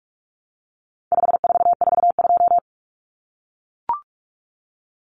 Morse code, heard as a keyed tone, about 44 wpm, 700 Hz. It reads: 5442
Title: Numbers - 4-Digit 45wpm